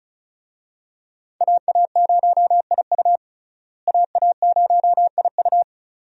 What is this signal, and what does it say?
Morse code, heard as a keyed tone, about 36 wpm, 700 Hz. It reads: AA0IU AA0IU
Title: USA Call Signs - 5 Characters - Repeated 2x - 35wpm